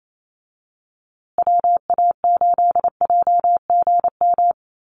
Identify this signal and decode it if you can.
Morse code, heard as a keyed tone, about 28 wpm, 700 Hz. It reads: WA8JGM
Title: USA Call Signs Repeated 2x 28wpm